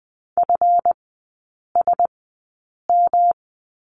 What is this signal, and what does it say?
Morse code, heard as a keyed tone, about 20 wpm, spaced as for 8 wpm, 690 Hz. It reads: FSM
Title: 159 Lesson 26e - Letter g - Three Characters 20wpm